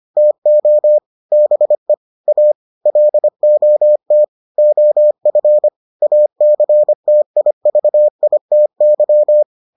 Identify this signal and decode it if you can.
Morse code, heard as a keyed tone, about 25 wpm, 600 Hz. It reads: TO BE A LOT OF ACTIVITY